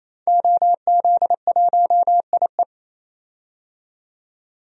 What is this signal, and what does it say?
Morse code, heard as a keyed tone, about 28 wpm, 700 Hz. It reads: OZ1IE
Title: DXCC-100 Call Signs - 5 Characters - 28wpm